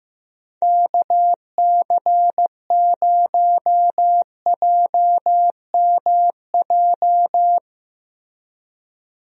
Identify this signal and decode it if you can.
Morse code, heard as a keyed tone, about 15 wpm, 700 Hz. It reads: KC0JMJ
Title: USA Call Signs - 6 Characters - Repeated 3x - 15wpm